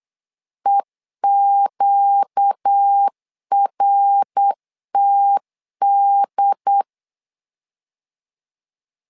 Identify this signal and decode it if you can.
Morse code, heard as a keyed tone, about 8 wpm, 800 Hz. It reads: EQRTD
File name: 4 Q 8wpm